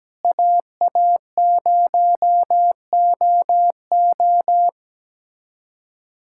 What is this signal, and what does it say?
Morse code, heard as a keyed tone, about 17 wpm, 700 Hz. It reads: AA0OO
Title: USA Callsign DE Callsign - 5 Characters - 17wpm